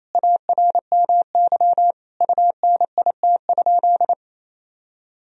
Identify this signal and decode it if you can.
Morse code, heard as a keyed tone, about 28 wpm, 700 Hz. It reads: ARMY UNIT?
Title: Sentences from Top 1000 Words - Review Part 2 - 28wpm